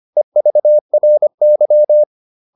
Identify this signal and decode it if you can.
Morse code, heard as a keyed tone, about 25 wpm, 600 Hz. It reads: EVRY
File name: SS319 25